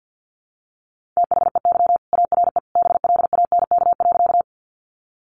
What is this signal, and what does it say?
Morse code, heard as a keyed tone, about 50 wpm, 700 Hz. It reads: THEY ARE BLANK.
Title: Sentences from Top 1500 Words 50wpm